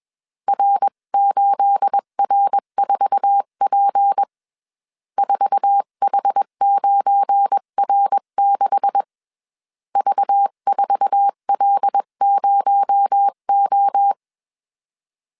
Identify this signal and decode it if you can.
Morse code, heard as a keyed tone, about 21 wpm, 800 Hz. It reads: R8R4P 4H9R6 V4L0O